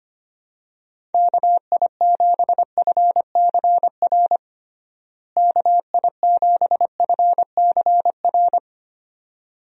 Morse code, heard as a keyed tone, about 25 wpm, 700 Hz. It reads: KI7FCR KI7FCR
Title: USA Call Signs - 6 Characters - Repeated 2x - 25wpm